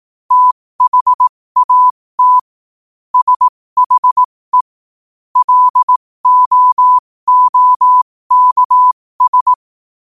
Morse code, heard as a keyed tone, about 18 wpm, spaced as for 15 wpm, 1000 Hz. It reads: THAT SHE LOOKS